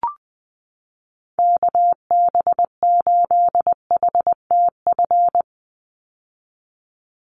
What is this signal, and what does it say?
Morse code, heard as a keyed tone, about 20 wpm, 700 Hz. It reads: KB8HTF
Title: USA Call Signs 20wpm